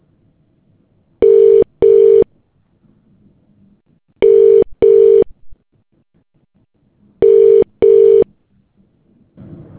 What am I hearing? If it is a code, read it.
Ringback tone (UK), 400 + 450 Hz.